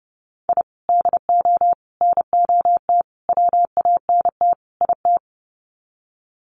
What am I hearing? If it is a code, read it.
Morse code, heard as a keyed tone, about 30 wpm, 700 Hz. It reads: I DO NOT WANT IT
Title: Sets of 5 Words Encore2 30wpm